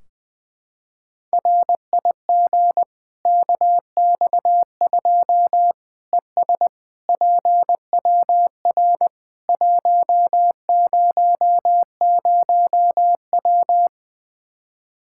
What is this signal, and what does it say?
Morse code, heard as a keyed tone, about 21 wpm, 700 Hz. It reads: RIG KX2 ES PWR 100W
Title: Practice QSO - Exchange 2 - Info - 20wpm